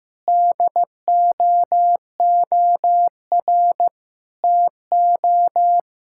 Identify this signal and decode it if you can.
Morse code, heard as a keyed tone, about 15 wpm, 700 Hz. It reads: DOOR TO
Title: Sentences from Top 1000 Words - Review Part 6 - 15wpm